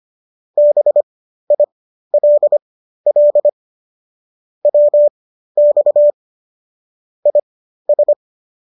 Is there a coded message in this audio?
Yes: Morse code from 0.6 to 8.1 s, heard as a keyed tone, about 25 wpm, spaced as for 13 wpm, 590 Hz. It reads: BILL WX IS